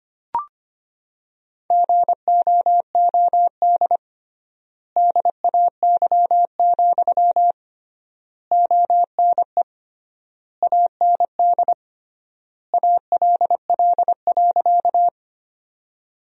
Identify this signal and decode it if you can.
Morse code, heard as a keyed tone, about 25 wpm, 700 Hz. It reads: GOOD DAY, ONE AND ALL.
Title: 3x Word-Spacing - Sentences from Top 200 Words - 25wpm